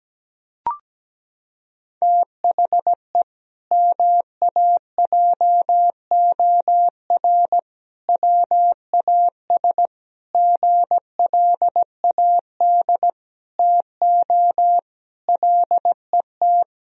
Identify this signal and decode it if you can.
Morse code, heard as a keyed tone, about 17 wpm, 700 Hz. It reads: THE MAJOR WAS GLAD TO LET
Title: Sentences from Top 1000 Words - Review Part 4 - 17wpm